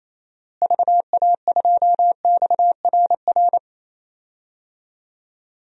Morse code, heard as a keyed tone, about 28 wpm, 700 Hz. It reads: VA2XRR
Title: DXCC-100 Call Signs - 6 Characters - Repeated 3x - 28wpm